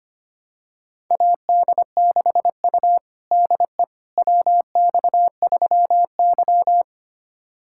Morse code, heard as a keyed tone, about 25 wpm, 700 Hz. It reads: AD6U DE WX3Y
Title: USA Callsign DE Callsign - 4 Characters - 25wpm